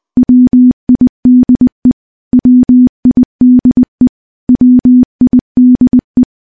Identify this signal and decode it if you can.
Morse code, heard as a keyed tone, about 20 wpm, 250 Hz. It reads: WIDE WIDE WIDE
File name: morse / morse-training-20WPM